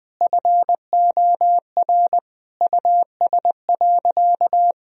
Morse code, heard as a keyed tone, about 20 wpm, 700 Hz. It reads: FOR US.
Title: Sentences from Top 1000 Words - Review Part 3 - 20wpm